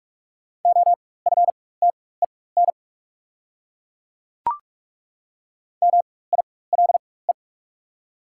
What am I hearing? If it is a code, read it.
Morse code, heard as a keyed tone, about 43 wpm, spaced as for 21 wpm, 690 Hz. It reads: OFTEN MILE
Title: 4x Character Spacing - Top 300 Words - 45wpm